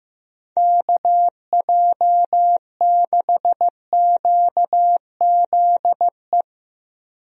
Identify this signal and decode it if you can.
Morse code, heard as a keyed tone, about 15 wpm, 700 Hz. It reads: KJ6QZE
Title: USA Call Signs - 6 Characters - Repeated 2x - 15wpm